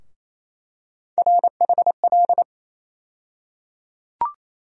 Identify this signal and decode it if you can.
Morse code, heard as a keyed tone, about 28 wpm, 700 Hz. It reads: RHL
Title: USA Call Sign Postfixes - 3 Letters 28wpm